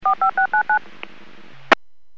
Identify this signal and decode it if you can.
DTMF, heard as telephone touch tones, tones of about 100 ms. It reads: 45699
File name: WPS / 03944 11-15 WPS 451.25 WPS